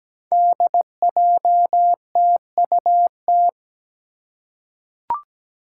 Morse code, heard as a keyed tone, about 17 wpm, 700 Hz. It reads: DJTUT
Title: Five Challenging Letter-Number Combinations 17wpm